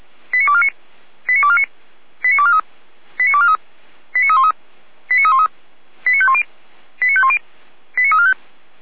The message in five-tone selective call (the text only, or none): 89249 89249 89243 89243 89313 89313 89510 89510 89345